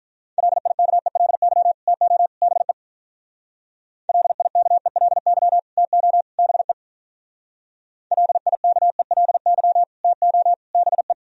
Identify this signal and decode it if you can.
Morse code, heard as a keyed tone, about 42 wpm, 700 Hz. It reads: LIKELY TO BE LIKELY TO BE LIKELY TO BE
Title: Speed-Racing - Sets of 3 Words 35wpm